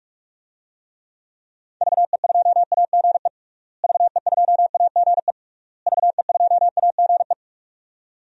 Morse code, heard as a keyed tone, about 42 wpm, 700 Hz. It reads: VE2AGE VE2AGE VE2AGE
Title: DXCC-100 Call Signs - 6 Characters - Repeated 3x - 45wpm